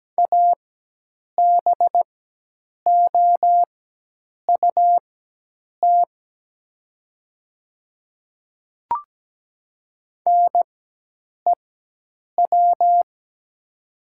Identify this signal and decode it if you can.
Morse code, heard as a keyed tone, about 17 wpm, spaced as for 8 wpm, 700 Hz. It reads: ABOUT NEW
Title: 4x Character Spacing - Top 300 Words - 17wpm